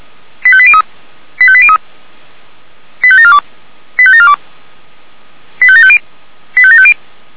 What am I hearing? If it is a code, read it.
Five-tone selective call (ZVEI), heard as quick runs of five tones, tones of about 70 ms. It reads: 85803 85803 85742 85742 85750 85750